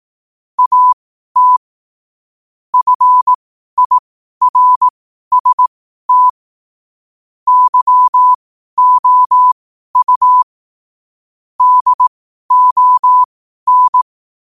Morse code, heard as a keyed tone, about 18 wpm, spaced as for 12 wpm, 1000 Hz. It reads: AT FIRST YOU DON
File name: qotdfiles / 12wpm